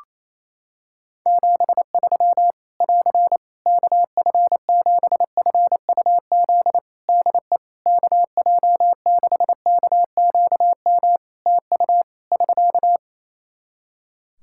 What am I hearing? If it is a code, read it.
Morse code, heard as a keyed tone, about 27 wpm, 700 Hz. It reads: 73 <AR> KF7FUZ DE KJ6KQM TU <SK>
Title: Practice QSO - Exchange 4 - Ending - 28wpm